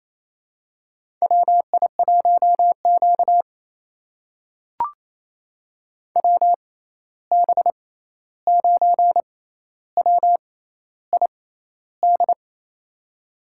Morse code, heard as a keyed tone, about 28 wpm, spaced as for 13 wpm, 700 Hz. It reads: WI1Q WB9WID
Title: USA Call Signs - 6x Character Spacing - 28wpm